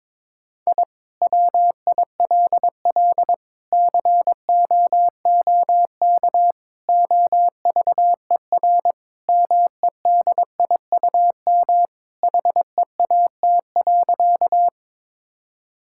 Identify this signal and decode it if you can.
Morse code, heard as a keyed tone, about 22 wpm, 700 Hz. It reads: I WILL COOK OVER MEDIUM HEAT.
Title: Sentences from Top 1000 Words - Review Part 6 - 22wpm